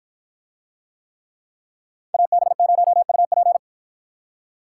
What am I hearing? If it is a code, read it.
Morse code, heard as a keyed tone, about 53 wpm, 700 Hz. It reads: AB0UP